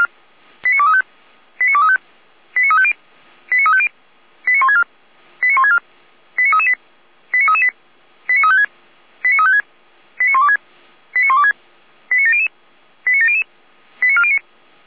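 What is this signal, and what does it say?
Five-tone selective call (ZVEI), heard as quick runs of five tones, tones of about 70 ms. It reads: 89235 89235 89350 89350 89164 89164 89338 89338 89356 89356 89126 89126 89700 89700 89409
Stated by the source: ZVEI five-tone selective call received on an SDR.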